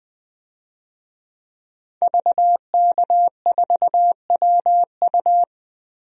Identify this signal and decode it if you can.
Morse code, heard as a keyed tone, about 21 wpm, 700 Hz. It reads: VK4WU
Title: DXCC-100 Call Signs - 5 Characters - Repeated 3x - 20wpm